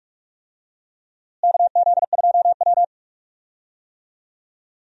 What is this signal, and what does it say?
Morse code, heard as a keyed tone, about 43 wpm, 700 Hz. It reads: KZ2W
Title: USA Call Signs - 4 Characters - Repeated 3x - 45wpm